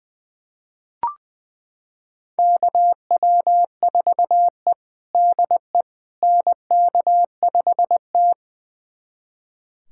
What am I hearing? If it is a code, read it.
Morse code, heard as a keyed tone, about 20 wpm, 700 Hz. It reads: KW4E DE NK5T
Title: USA Callsign DE Callsign - 4 Characters - 20wpm